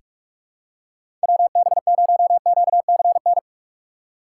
Morse code, heard as a keyed tone, about 41 wpm, 700 Hz. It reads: WB0QCN
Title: USA Call Signs - 6 Characters - 3x Character Spacing - 45wpm